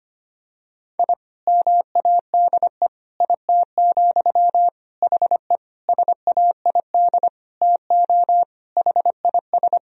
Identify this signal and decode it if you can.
Morse code, heard as a keyed tone, about 25 wpm, 700 Hz. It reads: I MADE IT, HE SAID TO HIS
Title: Sentences from Top 1000 Words - Review Part 1 - 25wpm